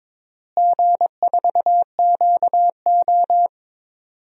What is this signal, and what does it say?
Morse code, heard as a keyed tone, about 22 wpm, 700 Hz. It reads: G4QO
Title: DXCC-100 Call Signs - 4 Characters - Repeated 3x - 22wpm